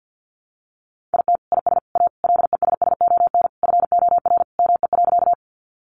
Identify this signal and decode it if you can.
Morse code, heard as a keyed tone, about 50 wpm, 700 Hz. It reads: IT IS A LESSON FOR ME.
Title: Sentences from Top 1200 Words 50wpm